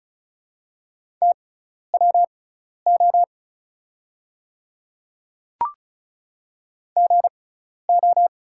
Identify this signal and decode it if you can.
Morse code, heard as a keyed tone, about 36 wpm, spaced as for 12 wpm, 700 Hz. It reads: TWO GO
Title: 6x Character Spacing - Top 100 Words - 35wpm